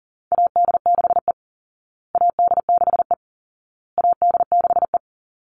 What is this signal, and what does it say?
Morse code, heard as a keyed tone, about 40 wpm, 700 Hz. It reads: AD6E AD6E AD6E
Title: Call Signs Repeated 2x - 40wpm